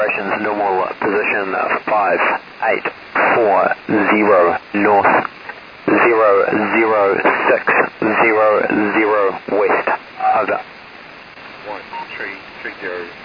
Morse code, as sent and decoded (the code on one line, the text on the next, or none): none